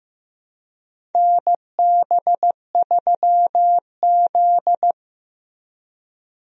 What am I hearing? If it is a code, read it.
Morse code, heard as a keyed tone, about 15 wpm, 700 Hz. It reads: NB3Z
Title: USA Call Signs - 4 Characters - Repeated 3x - 15wpm